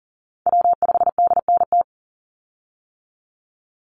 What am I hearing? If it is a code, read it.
Morse code, heard as a keyed tone, about 42 wpm, 700 Hz. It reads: W5DNT
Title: Call Signs - 40wpm